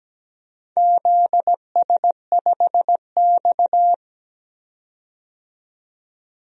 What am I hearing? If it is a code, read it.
Morse code, heard as a keyed tone, about 17 wpm, 700 Hz. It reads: ZS5X
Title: DXCC-100 Call Signs - 4 Characters - 17wpm